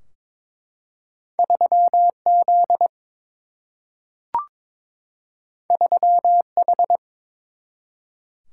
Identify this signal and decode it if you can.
Morse code, heard as a keyed tone, about 22 wpm, 700 Hz. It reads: 3 Z 3 H
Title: Single-Letter Single-Number 22wpm